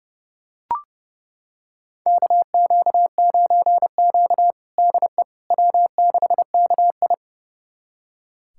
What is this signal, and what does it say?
Morse code, heard as a keyed tone, about 32 wpm, 700 Hz. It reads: KQ9Q DE W6KI
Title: USA Callsign DE Callsign - 4 Characters - 30wpm